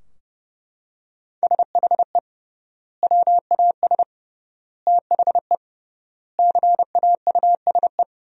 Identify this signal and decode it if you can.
Morse code, heard as a keyed tone, about 30 wpm, 700 Hz. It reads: SHE WAS THE CAUSE
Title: 3x Word-Spacing - Sentences from Top 200 Words - 30wpm